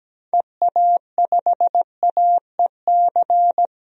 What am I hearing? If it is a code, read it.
Morse code, heard as a keyed tone, about 17 wpm, 700 Hz. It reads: EA5AEC
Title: DXCC-100 Call Signs - 6 Characters - 17wpm